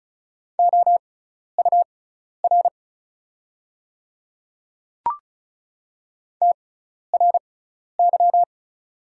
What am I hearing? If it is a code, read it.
Morse code, heard as a keyed tone, about 36 wpm, spaced as for 12 wpm, 690 Hz. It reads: OUR TRY
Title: 6x Character Spacing - Top 300 Words - 35wpm